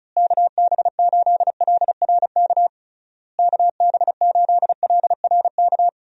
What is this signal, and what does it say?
Morse code, heard as a keyed tone, about 36 wpm, 700 Hz. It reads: KB8LRK KB8LRK
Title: USA Call Signs - 6 Characters - Repeated 3x - 35wpm